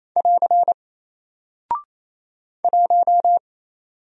Morse code, heard as a keyed tone, about 28 wpm, 700 Hz. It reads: <AR> 1
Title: Single Character (Letter:Number:Punctuation:Prosign) 28wpm